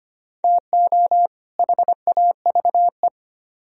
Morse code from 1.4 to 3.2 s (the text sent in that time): HAVE